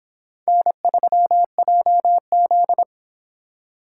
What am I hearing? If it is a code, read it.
Morse code, heard as a keyed tone, about 29 wpm, 700 Hz. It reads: N3JZ